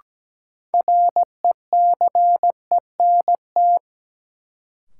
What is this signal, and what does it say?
Morse code, heard as a keyed tone, about 17 wpm, 700 Hz. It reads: RECENT
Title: Sentences from Top 1400 Words - New Word Drill Alt - 17wpm